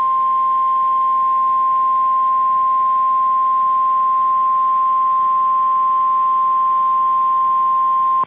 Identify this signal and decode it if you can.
Emergency Alert System (SAME) tones (the 1050 Hz attention tone). It is heard from 0.2 to 7.3 s.
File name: EAS Logs / KMNR